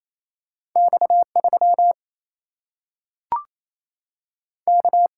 Morse code, heard as a keyed tone, about 28 wpm, 700 Hz. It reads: X 3 K
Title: Single-Letter Single-Number 28wpm